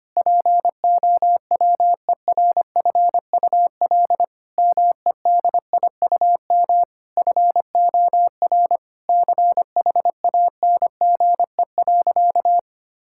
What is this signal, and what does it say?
Morse code, heard as a keyed tone, about 25 wpm, 700 Hz. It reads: POWERFUL MEDIUM FOR CHANGE.
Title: Sentences from Top 1000 Words - Review Part 6 - 25wpm